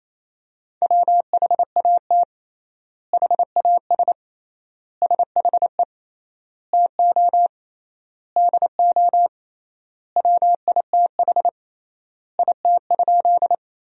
Morse code, heard as a keyed tone, about 28 wpm, 700 Hz. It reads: WHAT HAS SHE TO DO WITH IT?
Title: 3x Word-Spacing - Sentences from Top 100 Words - 28wpm